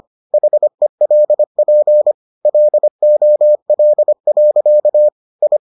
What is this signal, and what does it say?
Morse code, heard as a keyed tone, about 25 wpm, 600 Hz. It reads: HELP LOL. I